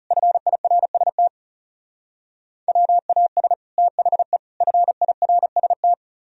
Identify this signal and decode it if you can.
Morse code, heard as a keyed tone, about 37 wpm, 700 Hz. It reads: FIRST WAS THE FIRST
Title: Speed-Racing - Sets of 3 Words 30wpm